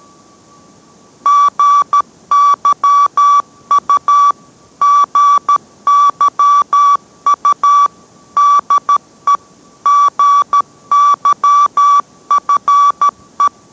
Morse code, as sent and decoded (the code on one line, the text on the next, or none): --. -.-- ..- / --. -.-- ..- / -.. / --. -.-- ..-. .
GYU GYU DE GYFE